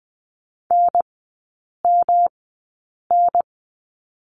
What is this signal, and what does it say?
Morse code, heard as a keyed tone, about 20 wpm, spaced as for 8 wpm, 700 Hz. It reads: NMN
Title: 107 Lesson 18e - Letter m - Three Characters 20wpm